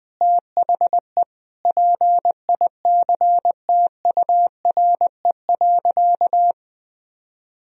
Morse code, heard as a keyed tone, about 20 wpm, 700 Hz. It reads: THE PICTURE.
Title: Sentences from Top 1000 Words - Review Part 2 - 20wpm